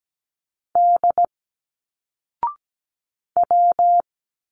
Morse code, heard as a keyed tone, about 17 wpm, 700 Hz. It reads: D W
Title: Single Letters 17wpm